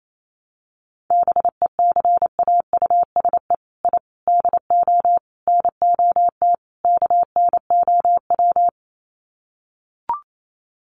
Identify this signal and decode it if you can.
Morse code, heard as a keyed tone, about 28 wpm, 700 Hz. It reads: BECAUSE I DO NOT KNOW
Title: Sets of 5 Words Encore2 28wpm